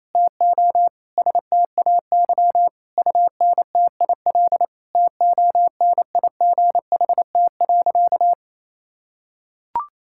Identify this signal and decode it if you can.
Morse code, heard as a keyed tone, about 28 wpm, 700 Hz. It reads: TO STAY UNTIL TONIGHT.
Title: Sentences from Top 1000 Words - Review Part 6 - 28wpm